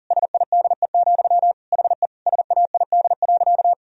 Morse code, heard as a keyed tone, about 40 wpm, 700 Hz. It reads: SIDE, HE SAID.
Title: Sentences from Top 1000 Words - Review Part 1 - 40wpm